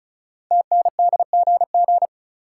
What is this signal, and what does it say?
Morse code, heard as a keyed tone, about 36 wpm, 690 Hz. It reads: TNDGG
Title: Code Groups - 5 Letters - 35wpm